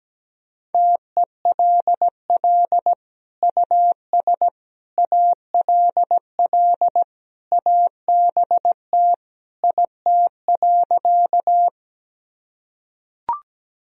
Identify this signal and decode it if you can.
Morse code, heard as a keyed tone, about 17 wpm, 700 Hz. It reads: TELL US ALL ABT IT.
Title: Sentences from Top 200 Words and Abbreviations 17wpm